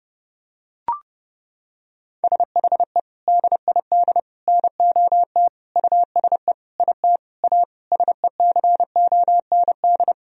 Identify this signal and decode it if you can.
Morse code, heard as a keyed tone, about 30 wpm, 700 Hz. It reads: SHE DID NOT USE IT A SECOND
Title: Sentences from Top 1000 Words - Review Part 1 - 30wpm